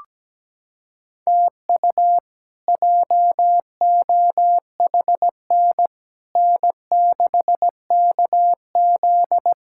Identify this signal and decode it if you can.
Morse code, heard as a keyed tone, about 17 wpm, 700 Hz. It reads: TU JOHN N6KZ
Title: K1USN SST QSO Practice - 17wpm